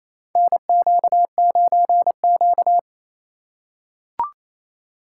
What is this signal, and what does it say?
Morse code, heard as a keyed tone, about 28 wpm, 700 Hz. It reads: NQ9Q
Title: USA Call Signs - 4 Characters - 4x Character Spacing - 28wpm